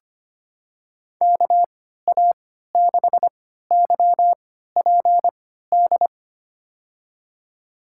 Morse code, heard as a keyed tone, about 25 wpm, spaced as for 14 wpm, 700 Hz. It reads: KA6YPD
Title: USA Call Signs - 3x Character Spacing - 25wpm